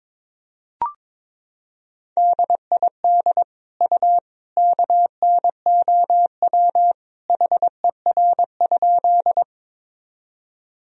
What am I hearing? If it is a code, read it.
Morse code, heard as a keyed tone, about 22 wpm, 700 Hz. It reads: DID U KNOW HER?
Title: Sentences from Top 100 Words and Abbreviations 22wpm